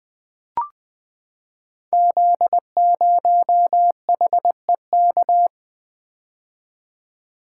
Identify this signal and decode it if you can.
Morse code, heard as a keyed tone, about 20 wpm, 700 Hz. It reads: Z0HEK
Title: Code Groups - 5 Letter:Numbers - 20wpm